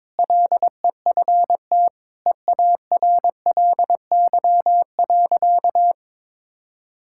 Morse code, heard as a keyed tone, about 21 wpm, 700 Hz. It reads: LEFT EARLY.
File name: practice / 22wpm / Sentences with 3-Word Phrases - 22wpm